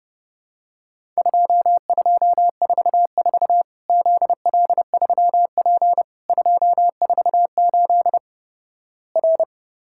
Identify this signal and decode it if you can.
Morse code, heard as a keyed tone, about 30 wpm, 700 Hz. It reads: 2244 ZL3P 248 R
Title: Worked All Europe DX Contest - QTC Exchange - 30wpm